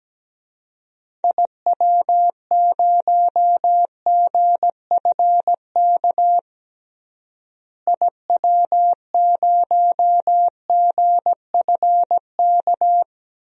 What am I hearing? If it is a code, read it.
Morse code, heard as a keyed tone, about 17 wpm, 700 Hz. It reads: IW0GFK IW0GFK
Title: DXCC-100 Call Signs - 6 Characters - Repeated 2x - 17wpm